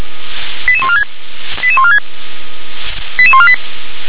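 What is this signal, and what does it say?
Five-tone selective call (ZVEI), heard as quick runs of five tones, tones of about 70 ms. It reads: 88146 88146 88148
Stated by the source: ZVEI five-tone selective call received on SDR.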